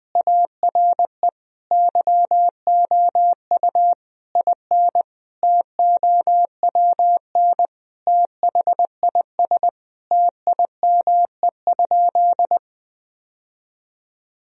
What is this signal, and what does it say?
Morse code, heard as a keyed tone, about 20 wpm, 700 Hz. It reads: ARE YOU IN TOWN THIS TIME?
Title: Sentences from Top 1000 Words - Review Part 2 - 20wpm